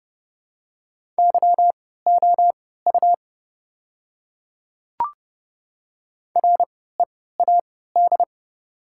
Morse code, heard as a keyed tone, about 30 wpm, spaced as for 17 wpm, 690 Hz. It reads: YOU READ
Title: 3x Character Spacing - Top 200 Words - 30wpm